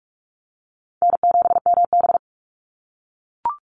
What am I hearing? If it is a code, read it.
Morse code, heard as a keyed tone, about 44 wpm, 700 Hz. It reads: N7MB
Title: Call Signs - 45wpm